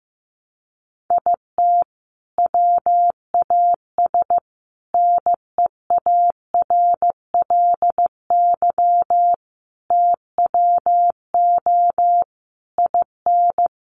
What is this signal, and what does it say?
Morse code, heard as a keyed tone, about 15 wpm, 700 Hz. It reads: IT WAS NEARLY TWO IN
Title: Sentences from Top 1300 Words 15wpm